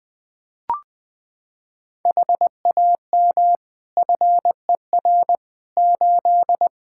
Morse code, heard as a keyed tone, about 20 wpm, 700 Hz. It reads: HAM FER 8
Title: Practice QSO - Exchange 3 - More Info - 20wpm